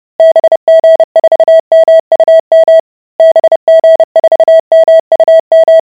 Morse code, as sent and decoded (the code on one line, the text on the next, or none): -... --. ....- -- ..- -- / -... --. ....- -- ..- --
BG4MUM BG4MUM